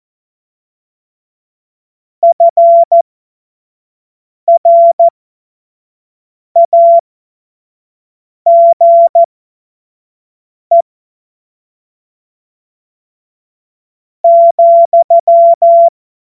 Morse code, heard as a keyed tone, about 14 wpm, spaced as for 4 wpm, 680 Hz. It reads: FRAGE,
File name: Morsekurs